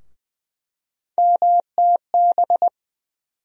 Morse code, heard as a keyed tone, about 20 wpm, 700 Hz. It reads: MTB
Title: USA Call Sign Postfixes - 3 Letters 20wpm